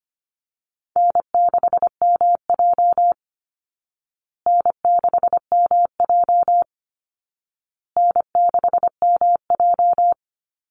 Morse code, heard as a keyed tone, about 25 wpm, 700 Hz. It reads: N6MJ N6MJ N6MJ
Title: Easy Call Signs Repeated 3x - 25wpm